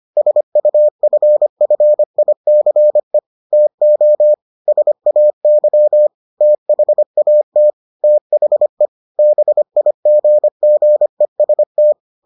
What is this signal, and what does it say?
Morse code, heard as a keyed tone, about 25 wpm, 600 Hz. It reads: SUFFICE TO SAY THAT THE BIGGEST